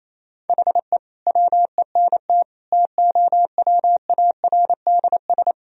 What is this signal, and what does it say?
Morse code, heard as a keyed tone, about 27 wpm, 700 Hz. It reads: HE WENT TOWARDS
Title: Sentences from Top 1000 Words - Review Part 2 - 28wpm